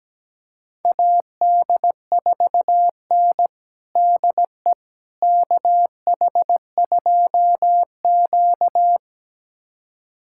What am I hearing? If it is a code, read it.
Morse code, heard as a keyed tone, about 17 wpm, 700 Hz. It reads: AD4N DE KH2Q
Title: USA Callsign DE Callsign - 4 Characters - 17wpm